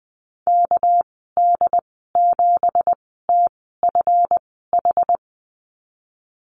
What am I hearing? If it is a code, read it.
Morse code, heard as a keyed tone, about 20 wpm, spaced as for 14 wpm, 700 Hz. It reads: KD7TFH
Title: 219 Lesson 35g - Number 0 - Call Signs 20wpm